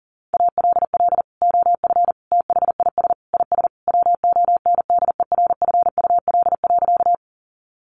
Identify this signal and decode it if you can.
Morse code, heard as a keyed tone, about 40 wpm, 700 Hz. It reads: ALL OF THIS IS WONDERFUL.
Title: Sentences from Top 1100 Words 40wpm